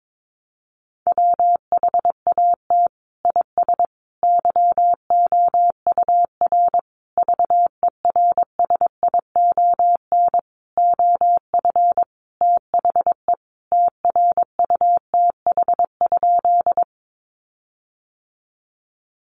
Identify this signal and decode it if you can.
Morse code, heard as a keyed tone, about 22 wpm, 700 Hz. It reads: WHAT IS YOUR VERSION OF THE TRUTH?
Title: Sentences from Top 1400 Words 22wpm